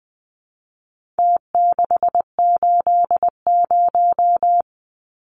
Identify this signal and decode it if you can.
Morse code, heard as a keyed tone, about 20 wpm, 700 Hz. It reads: T680
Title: Numbers - Contest Style - 4-Digit 20wpm